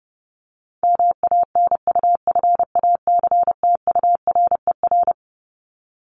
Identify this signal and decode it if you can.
Morse code, heard as a keyed tone, about 30 wpm, 700 Hz. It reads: MANUFACTURER
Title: Top 1751-2000 Words 30wpm